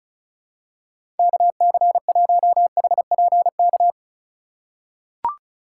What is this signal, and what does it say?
Morse code, heard as a keyed tone, about 36 wpm, 700 Hz. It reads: KC1HPK
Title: USA Call Signs - 4x Character Spacing - 35wpm